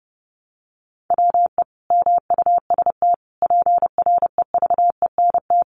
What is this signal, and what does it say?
Morse code, heard as a keyed tone, about 30 wpm, 700 Hz. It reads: WE MUST PREVENT
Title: Sentences from Top 1100 Words 30wpm